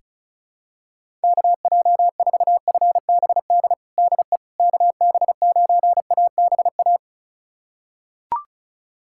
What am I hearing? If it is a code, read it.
Morse code, heard as a keyed tone, about 36 wpm, 700 Hz. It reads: KJ4FBD DE KB9ABA
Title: USA Callsign DE Callsign - 6 Characters - 35wpm